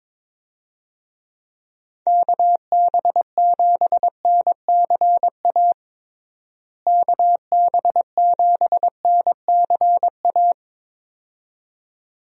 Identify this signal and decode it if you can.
Morse code, heard as a keyed tone, about 22 wpm, 700 Hz. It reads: KB7NCA KB7NCA
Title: USA Call Signs - 6 Characters - Repeated 2x - 22wpm